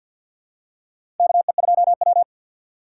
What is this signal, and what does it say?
Morse code, heard as a keyed tone, about 53 wpm, 700 Hz. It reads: KE2W
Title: USA Call Signs - 4 Characters - 3x Character Spacing - 50wpm